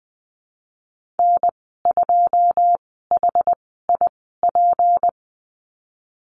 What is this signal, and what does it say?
Morse code, heard as a keyed tone, about 20 wpm, spaced as for 15 wpm, 700 Hz. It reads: N2HIP
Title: 154 Lesson 25g - Letter p - Call Signs 20wpm